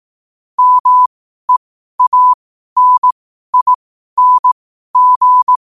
Morse code, heard as a keyed tone, about 18 wpm, spaced as for 12 wpm, 1000 Hz. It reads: MEANING
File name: qotdfiles / 12wpm